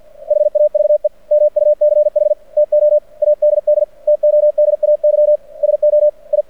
Morse code, heard as a keyed tone, about 48 wpm, 600 Hz. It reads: HAVE MUCH TO ADD TODAY SO I
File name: archived / CRSnet